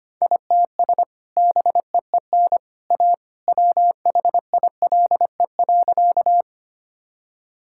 Morse code, heard as a keyed tone, about 25 wpm, 700 Hz. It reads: ITS BEEN A WHILE.
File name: practice / 25wpm / Sentences with 2-Word Phrases - 25wpm